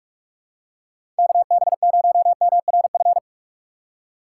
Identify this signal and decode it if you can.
Morse code, heard as a keyed tone, about 42 wpm, 700 Hz. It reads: KB0MRF
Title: USA Call Signs - 6 Characters - 6x Character Spacing - 45wpm